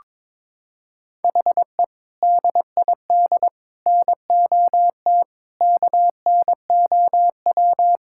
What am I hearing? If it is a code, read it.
Morse code, heard as a keyed tone, about 21 wpm, 700 Hz. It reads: HE DID NOT KNOW
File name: practice / 22wpm / Sentences with 3-Word Phrases - 22wpm